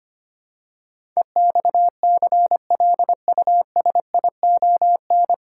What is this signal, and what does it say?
Morse code, heard as a keyed tone, about 25 wpm, 700 Hz. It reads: EXCLUSION
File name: Nine Letter Words - 25wpm